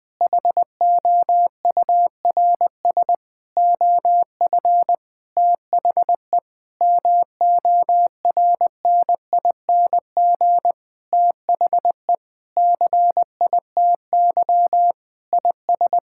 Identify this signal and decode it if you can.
Morse code, heard as a keyed tone, about 20 wpm, 700 Hz. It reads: HOURS OF THE MORNING THE CITY IS
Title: Sentences with 7-Word Phrases - Spoken Beginning - 20wpm